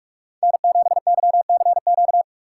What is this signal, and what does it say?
Morse code, heard as a keyed tone, about 46 wpm, 700 Hz. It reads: N7YCQ